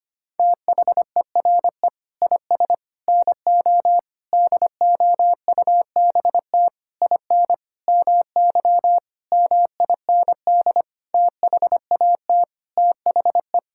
Morse code, heard as a keyed tone, about 25 wpm, 700 Hz. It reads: THERE IS NO DOUBT IN MY MIND THAT THE
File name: practice / 25wpm / Sentences with 9-Word Phrases - 25wpm